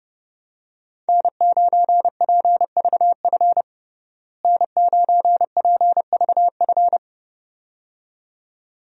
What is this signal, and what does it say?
Morse code, heard as a keyed tone, about 32 wpm, 700 Hz. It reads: N9PVF N9PVF
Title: USA Call Signs - 5 Characters - Repeated 2x - 30wpm